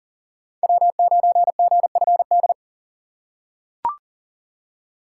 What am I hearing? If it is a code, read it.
Morse code, heard as a keyed tone, about 37 wpm, 700 Hz. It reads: W9GFD
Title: USA Call Signs - 5 Characters - 4x Character Spacing - 40wpm